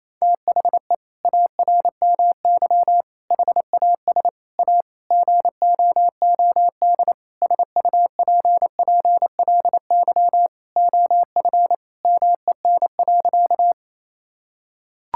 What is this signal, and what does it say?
Morse code, heard as a keyed tone, about 28 wpm, 700 Hz. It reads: THE ARMY HAS A GOOD SUPPLY OF MEN.
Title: Sentences from Top 1000 Words - Review Part 5 - 28wpm